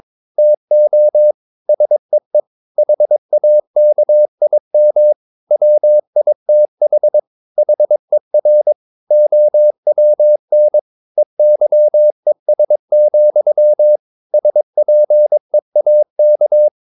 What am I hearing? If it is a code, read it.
Morse code, heard as a keyed tone, about 22 wpm, 600 Hz. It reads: TO SEE HAKIM WITH HER OWN EYES, SPEAK
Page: Romanen Never af Ken Follet på engelsk.